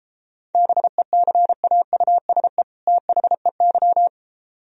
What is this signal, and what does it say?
Morse code, heard as a keyed tone, about 33 wpm, 700 Hz. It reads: BECAUSE THEY